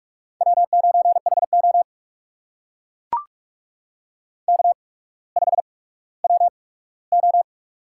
Morse code, heard as a keyed tone, about 44 wpm, spaced as for 19 wpm, 700 Hz. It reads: W9HO K5WO
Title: USA Call Signs - 4 Characters - 8x Character Spacing - 45wpm